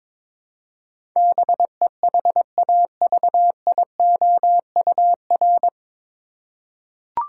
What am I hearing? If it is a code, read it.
Morse code, heard as a keyed tone, about 22 wpm, 700 Hz. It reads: BEHAVIOUR
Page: Morse Code Ninja practice 22wpm